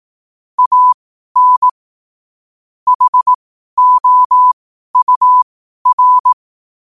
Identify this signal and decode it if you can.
Morse code, heard as a keyed tone, about 18 wpm, spaced as for 12 wpm, 1000 Hz. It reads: AN HOUR
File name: qotdfiles / 12wpm